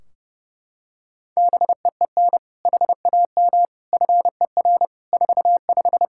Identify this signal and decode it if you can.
Morse code, heard as a keyed tone, about 32 wpm, 700 Hz. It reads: BEEN HAM FER 45
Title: Practice QSO - Exchange 3 - More Info - 30wpm